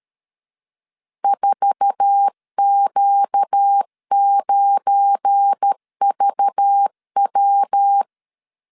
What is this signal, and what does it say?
Morse code, heard as a keyed tone, about 13 wpm, 800 Hz. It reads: 4Q9VW
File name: CW / DET / 1 9 1500C S B 12wpm 3819